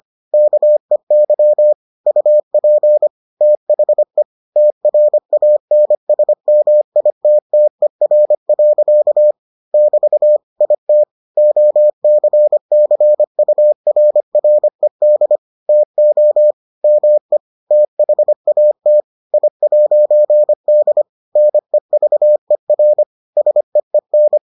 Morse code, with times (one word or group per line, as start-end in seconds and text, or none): 0.3-1.7 KEY
2.1-3.1 UP
3.4-4.2 THE
4.6-9.3 TRANSMITTER.
9.7-10.4 <BT>
10.6-11.0 IT
11.4-15.4 OCCURRED
15.7-16.5 TO
16.8-17.4 ME
17.7-19.0 THAT
19.3-21.0 I'D
21.4-23.0 NEVER
23.4-24.4 SEEN